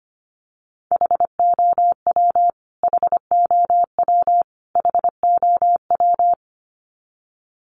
Morse code, heard as a keyed tone, about 25 wpm, 700 Hz. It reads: HOW HOW HOW
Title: Top 500 Words Repeated 25wpm